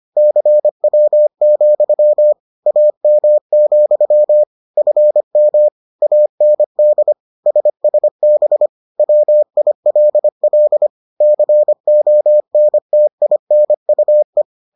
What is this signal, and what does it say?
Morse code, heard as a keyed tone, about 25 wpm, 600 Hz. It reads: CW, AM, FM AND SSB WILL CONTINUE